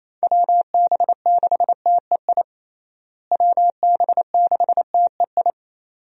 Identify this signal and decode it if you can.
Morse code, heard as a keyed tone, about 27 wpm, 700 Hz. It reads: WB6TEI WB6TEI
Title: USA Call Signs - 6 Characters - Repeated 2x - 28wpm